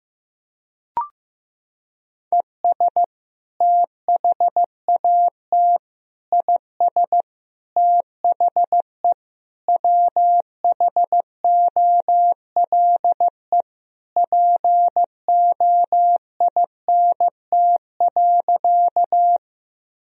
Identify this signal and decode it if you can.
Morse code, heard as a keyed tone, about 16 wpm, 700 Hz. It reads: ES THAT IS THE WHOLE POINT.
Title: Sentences from Top 400 Words and Abbreviations 15wpm